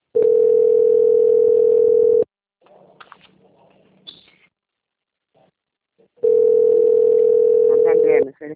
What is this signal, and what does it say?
Ringback tone (North America), 440 + 480 Hz, with speech.
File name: out- - - - - .1896